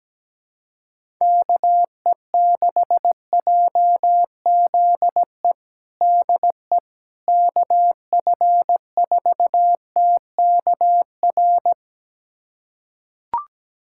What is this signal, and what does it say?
Morse code, heard as a keyed tone, about 17 wpm, 700 Hz. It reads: KE6JZE DE KF4TKR
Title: USA Callsign DE Callsign - 6 Characters - 17wpm